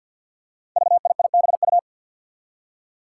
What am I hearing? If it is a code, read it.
Morse code, heard as a keyed tone, about 52 wpm, 700 Hz. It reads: VIIBU
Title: Five Challenging Letter-Number Combinations 50wpm